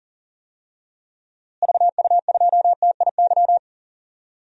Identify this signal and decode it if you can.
Morse code, heard as a keyed tone, about 40 wpm, 700 Hz. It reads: VU2TIY
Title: DXCC-100 Call Signs - 6 Characters - Repeated 3x - 40wpm